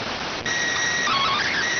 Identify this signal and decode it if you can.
SSTV (slow-scan television), PD 290.